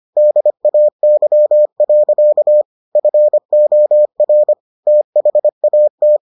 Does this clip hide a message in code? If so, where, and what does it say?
Yes: Morse code from 0.2 to 6.2 s, heard as a keyed tone, about 25 wpm, 600 Hz. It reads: DAY. FOR THAT